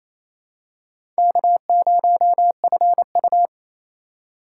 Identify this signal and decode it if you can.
Morse code, heard as a keyed tone, about 28 wpm, 690 Hz. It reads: K0FU
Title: USA Call Signs - 3x Character Spacing - 28wpm